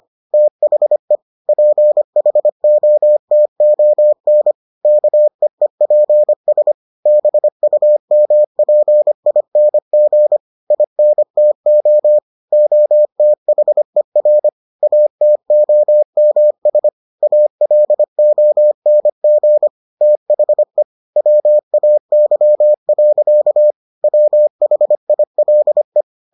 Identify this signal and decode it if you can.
Morse code, heard as a keyed tone, about 25 wpm, 600 Hz. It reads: THE PHOTON KEEPS BUMPING INTO OTHER ATOMS ALONG THE WAY. WHILE